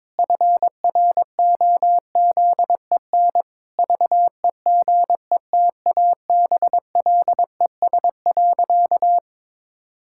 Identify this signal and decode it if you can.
Morse code, heard as a keyed tone, about 22 wpm, 700 Hz. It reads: FROZEN VEGETABLES.
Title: Pangram Challenge v2 - 22wpm